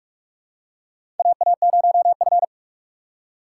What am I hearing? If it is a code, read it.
Morse code, heard as a keyed tone, about 46 wpm, 700 Hz. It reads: AA0F
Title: USA Call Signs - 4 Characters - 3x Character Spacing - 45wpm